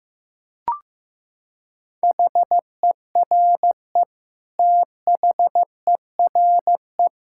Morse code, heard as a keyed tone, about 15 wpm, 700 Hz. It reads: HERE THERE
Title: Sentences from Top 1600 Words 15wpm